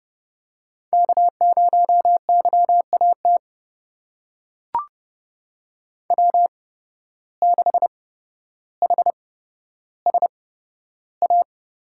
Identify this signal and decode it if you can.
Morse code, heard as a keyed tone, about 30 wpm, spaced as for 13 wpm, 700 Hz. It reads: K0YAT W6HSA
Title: USA Call Signs - 5 Characters - 8x Character Spacing - 30wpm